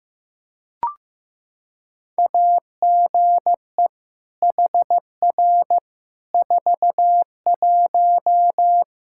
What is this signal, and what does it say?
Morse code, heard as a keyed tone, about 16 wpm, 700 Hz. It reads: AGE HR 41
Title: Practice QSO - Exchange 3 - More Info - 15wpm